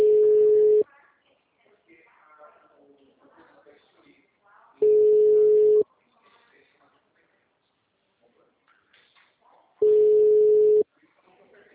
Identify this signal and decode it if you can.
Ringback tone (Europe, 425 Hz), with speech.
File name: out- -2002- - - .9351